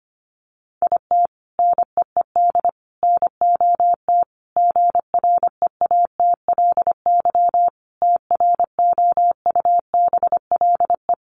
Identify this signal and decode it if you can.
Morse code, heard as a keyed tone, about 25 wpm, 700 Hz. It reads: IT NEED NOT GREATLY TROUBLE
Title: Sentences from Top 1400 Words 25wpm